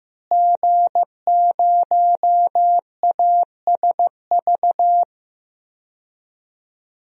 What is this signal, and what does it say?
Morse code, heard as a keyed tone, about 15 wpm, 700 Hz. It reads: G0ASV
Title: DXCC-100 Call Signs - 5 Characters - Repeated 3x - 15wpm